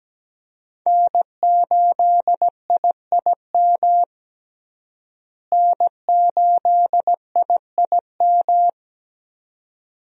Morse code, heard as a keyed tone, about 17 wpm, 700 Hz. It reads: N8IIM N8IIM
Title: USA Call Signs - 5 Characters - Repeated 2x - 17wpm